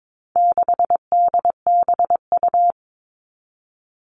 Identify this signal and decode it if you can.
Morse code, heard as a keyed tone, about 22 wpm, 700 Hz. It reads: 6DBU
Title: All Cut Numbers - 4-Digit 22wpm